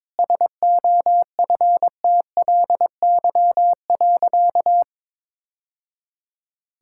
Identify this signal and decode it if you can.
Morse code, heard as a keyed tone, about 22 wpm, 690 Hz. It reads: SOFTLY.
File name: practice / 22wpm / Easy Sentences with Hints - 22wpm